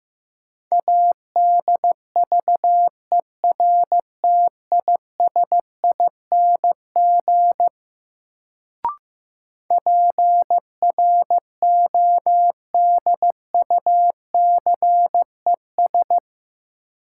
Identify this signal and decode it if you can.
Morse code, heard as a keyed tone, about 15 wpm, 700 Hz. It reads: ADVERTISING PRODUCES
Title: Sentences from Top 1000 Words - Review Part 6 - New Word Drill - 15wpm